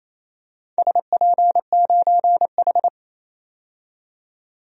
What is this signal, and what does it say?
Morse code, heard as a keyed tone, about 28 wpm, 700 Hz. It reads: SP9H
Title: DXCC-100 Call Signs - 4 Characters - 28wpm